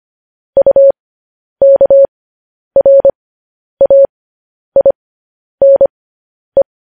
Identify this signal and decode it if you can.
Morse code, heard as a keyed tone, about 25 wpm, spaced as for 10 wpm, 550 Hz. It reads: UKRAINE
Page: news-headlines-in-morse-code-at-10-wpm